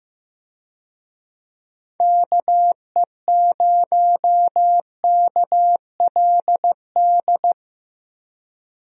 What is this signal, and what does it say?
Morse code, heard as a keyed tone, about 15 wpm, 700 Hz. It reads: KE0KLD
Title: USA Call Signs - 6 Characters - 15wpm